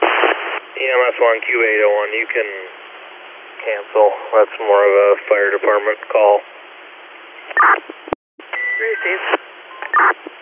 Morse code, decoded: none

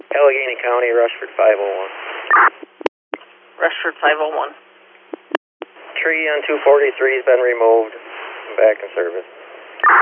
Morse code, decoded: none

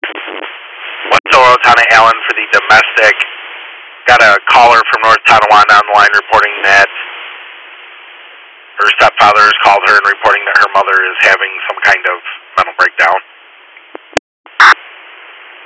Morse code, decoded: none

